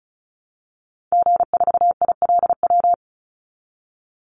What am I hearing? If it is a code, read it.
Morse code, heard as a keyed tone, about 34 wpm, 700 Hz. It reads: G4ILW
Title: Call Signs Repeated 2x - 35wpm